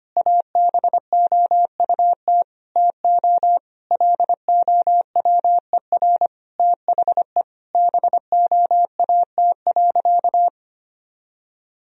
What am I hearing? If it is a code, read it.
Morse code, heard as a keyed tone, about 25 wpm, 700 Hz. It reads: ABOUT TO LOWER THE BOAT.
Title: Sentences from Top 1000 Words - Review Part 4 - 25wpm